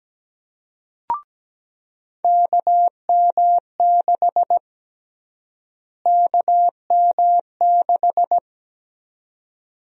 Morse code, heard as a keyed tone, about 17 wpm, 700 Hz. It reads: KM6 KM6
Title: USA Call Sign Prefixes - Repeated 2x - 17wpm